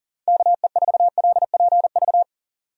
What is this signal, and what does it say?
Morse code, heard as a keyed tone, about 39 wpm, 700 Hz. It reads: KE4LPV
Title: USA Call Signs - 6 Characters - 40wpm